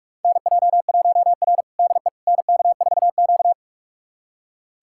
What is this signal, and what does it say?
Morse code, heard as a keyed tone, about 41 wpm, 700 Hz. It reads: NJ1R DE NK4Q
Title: USA Callsign DE Callsign - 4 Characters - 45wpm